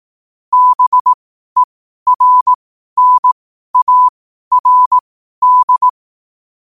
Morse code, heard as a keyed tone, about 18 wpm, spaced as for 12 wpm, 1000 Hz. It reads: BERNARD
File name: qotdfiles / 12wpm